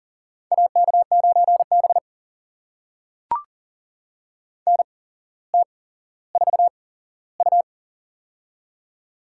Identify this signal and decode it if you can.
Morse code, heard as a keyed tone, about 40 wpm, spaced as for 17 wpm, 700 Hz. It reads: AK9B NT4U
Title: USA Call Signs - 4 Characters - 8x Character Spacing - 40wpm